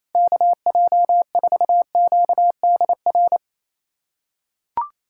Morse code, heard as a keyed tone, about 27 wpm, 700 Hz. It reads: KJ4QDR
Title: USA Callsign DE Callsign - 6 Characters - 28wpm